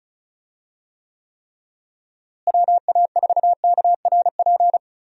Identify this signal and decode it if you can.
Morse code, heard as a keyed tone, about 36 wpm, 700 Hz. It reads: WA4KRP
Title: USA Call Signs - 6 Characters - Repeated 3x - 35wpm